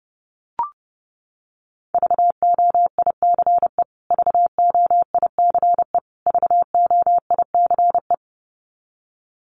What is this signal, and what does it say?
Morse code, heard as a keyed tone, about 30 wpm, 700 Hz. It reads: VOICE VOICE VOICE
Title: Top 500 Words Repeated 30wpm